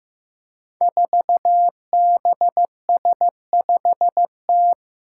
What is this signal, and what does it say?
Morse code, heard as a keyed tone, about 15 wpm, 700 Hz. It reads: 4BS5T
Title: Five Challenging Letter-Number Combinations 15wpm